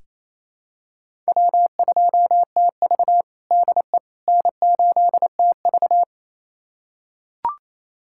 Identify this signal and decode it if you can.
Morse code, heard as a keyed tone, about 28 wpm, 700 Hz. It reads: W2TV DE N8TV
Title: USA Callsign DE Callsign - 4 Characters - 28wpm